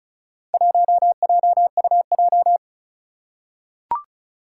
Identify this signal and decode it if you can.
Morse code, heard as a keyed tone, about 36 wpm, 690 Hz. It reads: 1JUJ
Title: Four Challenging Letter-Number Combinations 35wpm